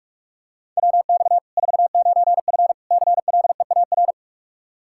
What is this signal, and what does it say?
Morse code, heard as a keyed tone, about 44 wpm, 700 Hz. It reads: WX 49F CLEAR
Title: Practice QSO - Exchange 2 - Info - 45wpm